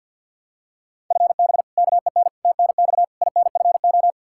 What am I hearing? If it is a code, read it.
Morse code, heard as a keyed tone, about 53 wpm, 700 Hz. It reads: FB KEN TNX INFO
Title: Practice QSO - Exchange 3 - More Info - 50wpm